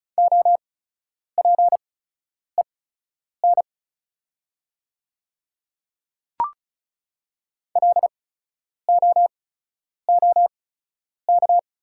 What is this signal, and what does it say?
Morse code, heard as a keyed tone, about 36 wpm, spaced as for 10 wpm, 690 Hz. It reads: OPEN LOOK
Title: 8x Character Spacing - Top 300 Words - 35wpm